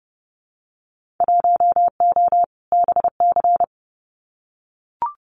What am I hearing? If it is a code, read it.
Morse code, heard as a keyed tone, about 30 wpm, 700 Hz. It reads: 1O BC
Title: Winter Field Day Exchanges Encore 30wpm